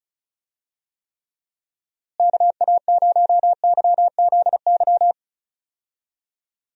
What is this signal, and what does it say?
Morse code, heard as a keyed tone, about 36 wpm, 700 Hz. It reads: KA0YZY
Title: USA Call Signs - 6 Characters - Repeated 3x - 35wpm